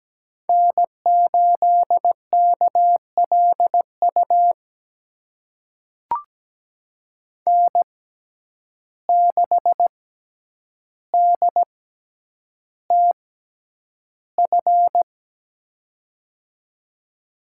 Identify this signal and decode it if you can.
Morse code, heard as a keyed tone, about 17 wpm, spaced as for 9 wpm, 700 Hz. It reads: N8KLU N6DTF
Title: USA Call Signs - 5 Characters - 6x Character Spacing - 17wpm